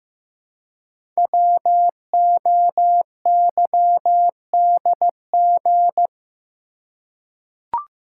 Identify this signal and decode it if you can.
Morse code, heard as a keyed tone, about 15 wpm, 700 Hz. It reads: WOYDG
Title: Code Groups - 5 Letters - 15wpm